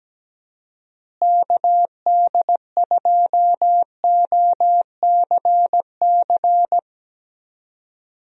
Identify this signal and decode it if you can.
Morse code, heard as a keyed tone, about 17 wpm, 700 Hz. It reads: KD2OCC
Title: USA Call Signs - 6 Characters - Repeated 2x - 17wpm